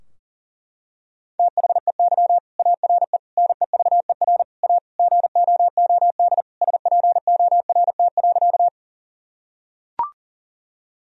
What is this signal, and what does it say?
Morse code, heard as a keyed tone, about 40 wpm, 700 Hz. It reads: THEY ARE NEVER A GOOD SPORT.
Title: Sentences from Top 1000 Words - Review Part 4 - 40wpm